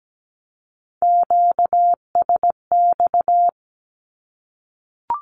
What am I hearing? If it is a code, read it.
Morse code, heard as a keyed tone, about 17 wpm, 700 Hz. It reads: QSX
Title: Q-Codes Repeated 3x 17wpm